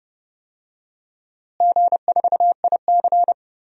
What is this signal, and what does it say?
Morse code, heard as a keyed tone, about 30 wpm, 700 Hz. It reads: G4IC
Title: DXCC-100 Call Signs - 4 Characters - Repeated 3x - 30wpm